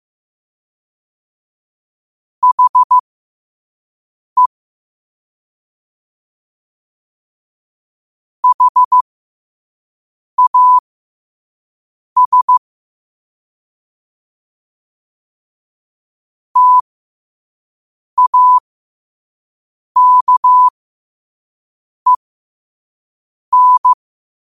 Morse code, heard as a keyed tone, about 15 wpm, spaced as for 5 wpm, 1000 Hz. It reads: HE HAS TAKEN
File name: qotdfiles / 5wpm